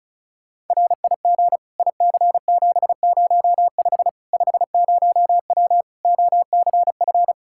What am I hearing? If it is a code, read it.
Morse code, heard as a keyed tone, about 36 wpm, 700 Hz. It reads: RIG IC705 50W OCF
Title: Practice QSO - Exchange 2 - Info - 35wpm